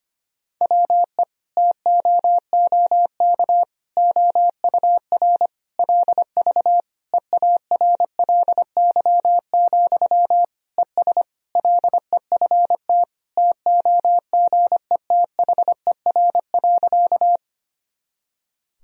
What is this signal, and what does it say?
Morse code, heard as a keyed tone, about 25 wpm, 700 Hz. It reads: WE TOOK OUR LV EARLY, ES LEFT TOGETHER.
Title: Sentences from Top 400 Words and Abbreviations 25wpm